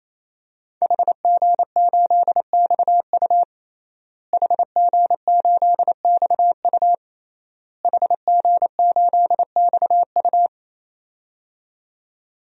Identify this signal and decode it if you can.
Morse code, heard as a keyed tone, about 28 wpm, 700 Hz. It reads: HG8XU HG8XU HG8XU
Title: DXCC-100 Call Signs - 5 Characters - Repeated 3x - 28wpm